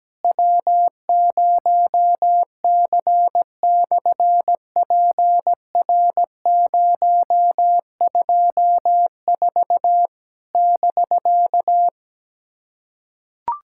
Morse code, heard as a keyed tone, about 17 wpm, 700 Hz. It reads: W0C/PR024 <BK>
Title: SOTA QSO Practice 17wpm